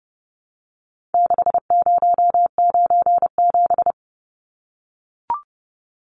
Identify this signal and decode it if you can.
Morse code, heard as a keyed tone, about 29 wpm, 700 Hz. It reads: 6097
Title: Numbers - 4-Digit 30wpm